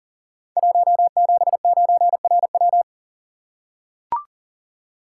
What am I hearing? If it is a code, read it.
Morse code, heard as a keyed tone, about 42 wpm, 700 Hz. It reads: 179RW
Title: Code Groups - 5 Letter:Numbers - 40wpm